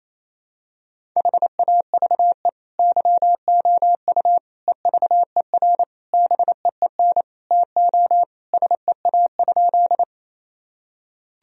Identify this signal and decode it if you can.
Morse code, heard as a keyed tone, about 28 wpm, 700 Hz. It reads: HAVE YOU EVER BEEN TO SEA?
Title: Sentences from Top 1000 Words - Review Part 1 - 28wpm